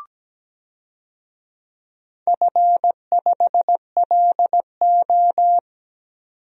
Morse code, heard as a keyed tone, about 17 wpm, 700 Hz. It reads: F5LO
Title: DXCC-100 Call Signs - 4 Characters - 17wpm